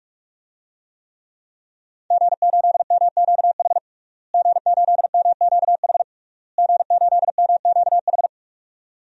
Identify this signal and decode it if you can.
Morse code, heard as a keyed tone, about 46 wpm, 700 Hz. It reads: G8MQH G8MQH G8MQH
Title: DXCC-100 Call Signs - 5 Characters - Repeated 3x - 45wpm